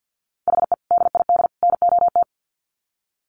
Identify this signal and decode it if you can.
Morse code, heard as a keyed tone, about 48 wpm, 700 Hz. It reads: HE DID NOT
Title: Sets of 3 Words 50wpm